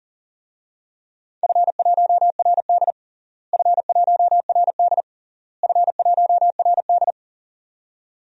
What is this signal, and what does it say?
Morse code, heard as a keyed tone, about 40 wpm, 700 Hz. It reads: F1RD F1RD F1RD
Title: DXCC-100 Call Signs - 4 Characters - Repeated 3x - 40wpm